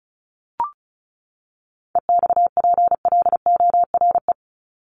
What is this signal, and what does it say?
Morse code, heard as a keyed tone, about 34 wpm, 700 Hz. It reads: EXPLORE
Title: Top 1001-1250 Words 35wpm